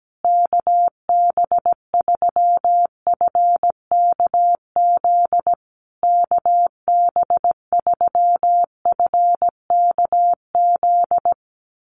Morse code, heard as a keyed tone, about 17 wpm, 700 Hz. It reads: KB3FKZ KB3FKZ
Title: USA Call Signs Repeated 3x 17wpm